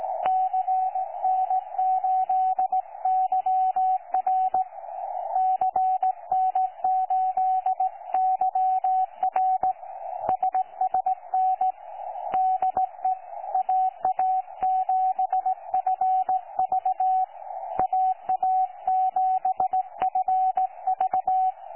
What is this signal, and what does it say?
Morse code, heard as a keyed tone, about 17 wpm, 750 Hz, with radio noise. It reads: CN8YR CN8YR SSN DE AA7FV AA7FV
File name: 12 39Z 18088.2kHz utah aa7fv k9tjl